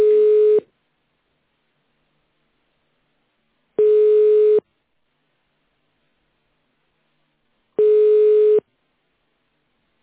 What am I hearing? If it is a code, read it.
Ringback tone (425 Hz, 1 s on, 3 s off).